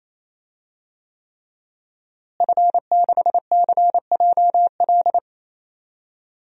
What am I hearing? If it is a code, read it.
Morse code, heard as a keyed tone, about 27 wpm, 700 Hz. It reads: F6CJL
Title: DXCC-100 Call Signs - 5 Characters - Repeated 3x - 28wpm